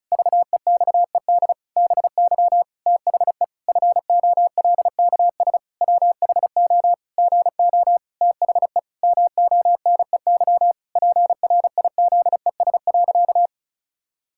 Morse code, heard as a keyed tone, about 34 wpm, 700 Hz. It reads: VEXED BY THE FOLKS WHO GO THE MONEY PRIZES.
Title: Pangram Challenge v2 - 35wpm